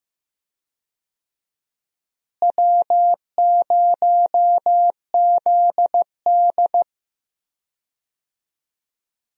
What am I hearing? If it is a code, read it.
Morse code, heard as a keyed tone, about 15 wpm, 700 Hz. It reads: W0ZD
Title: USA Call Signs - 4 Characters - Repeated 2x - 15wpm